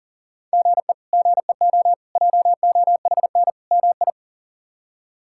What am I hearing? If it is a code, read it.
Morse code, heard as a keyed tone, about 40 wpm, 700 Hz. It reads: GE GEO JOHN MI
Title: K1USN SST QSO Practice - 40wpm